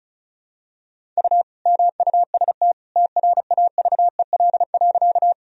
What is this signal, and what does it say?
Morse code, heard as a keyed tone, about 34 wpm, 700 Hz. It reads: U MUST TRAVEL.
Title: Sentences from Top 400 Words and Abbreviations 35wpm